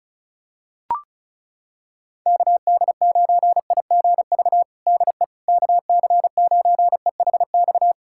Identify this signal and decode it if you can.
Morse code, heard as a keyed tone, about 34 wpm, 700 Hz. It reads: KD9IGV DE KC9EHX
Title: USA Callsign DE Callsign - 6 Characters - 35wpm